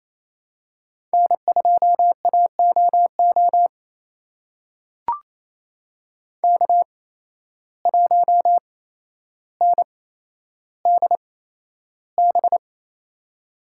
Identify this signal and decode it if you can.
Morse code, heard as a keyed tone, about 28 wpm, spaced as for 12 wpm, 700 Hz. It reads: N2AOO K1NDB
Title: USA Call Signs - 5 Characters - 8x Character Spacing - 28wpm